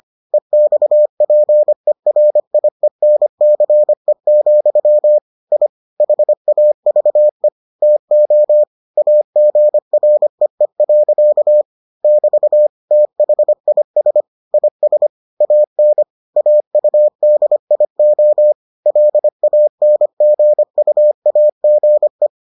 Morse code, heard as a keyed tone, about 25 wpm, 600 Hz. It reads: EXPERIENCE, I HAVE TO AGREE. <BT> THIS IS AN AUDIO LANGUAGE